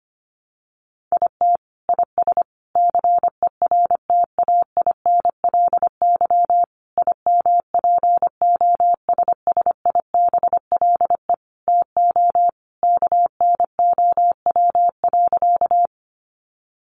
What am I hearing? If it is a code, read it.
Morse code, heard as a keyed tone, about 25 wpm, 700 Hz. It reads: IT IS CERTAINLY IMPOSSIBLE TO KNOW.
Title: Sentences from Top 1100 Words 25wpm